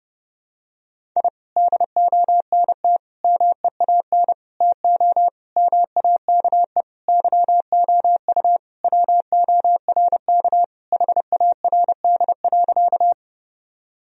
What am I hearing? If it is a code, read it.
Morse code, heard as a keyed tone, about 30 wpm, 700 Hz. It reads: I DONT MEAN TO MAKE YOU WORK HARD.
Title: Sentences from Top 1000 Words - Review Part 1 - 30wpm